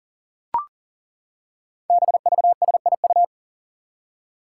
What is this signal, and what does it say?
Morse code, heard as a keyed tone, about 40 wpm, 700 Hz. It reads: BVSIU
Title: Five Challenging Letter-Number Combinations 40wpm